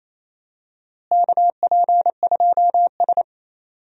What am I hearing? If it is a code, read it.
Morse code, heard as a keyed tone, about 28 wpm, 700 Hz. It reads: KP2S
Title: DXCC-100 Call Signs - 4 Characters - Repeated 3x - 28wpm